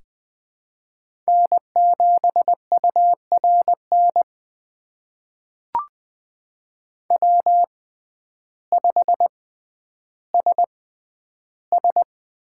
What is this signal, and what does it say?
Morse code, heard as a keyed tone, about 20 wpm, spaced as for 11 wpm, 700 Hz. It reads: N7URN W5SS
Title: USA Call Signs - 5 Characters - 6x Character Spacing - 20wpm